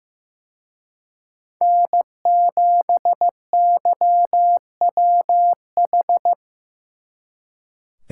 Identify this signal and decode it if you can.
Morse code, heard as a keyed tone, about 17 wpm, 700 Hz. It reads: N7YWH
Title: Speed Racing - USA 1x3 Call Signs 15wpm